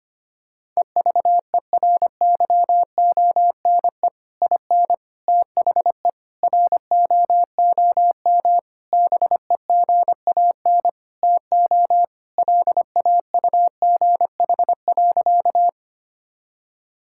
Morse code, heard as a keyed tone, about 25 wpm, 700 Hz. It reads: EVERYONE IN THE ROOM BEGAN TO LAUGH.
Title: Sentences from Top 1000 Words - Review Part 5 - 25wpm